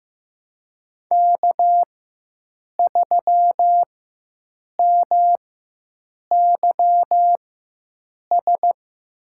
Morse code, heard as a keyed tone, about 15 wpm, spaced as for 7 wpm, 700 Hz. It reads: K3MYS
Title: USA Call Signs - 5 Characters - 4x Character Spacing - 15wpm